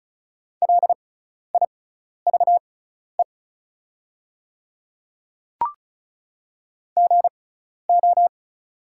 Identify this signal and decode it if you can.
Morse code, heard as a keyed tone, about 35 wpm, spaced as for 12 wpm, 690 Hz. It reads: LIVE GO
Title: 6x Character Spacing - Top 200 Words - 35wpm